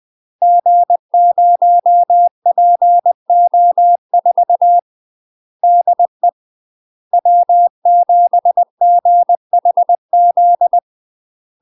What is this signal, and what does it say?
Morse code, heard as a keyed tone, about 20 wpm, 700 Hz. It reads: G0PO4 DE W7GHZ
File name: UK MASTER QSOs TRAINING / 20WPM / UK MASTER QSOs 0067